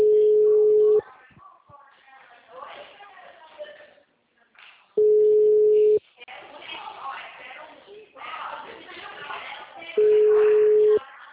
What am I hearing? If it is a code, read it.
Ringback tone (Europe, 425 Hz), with speech.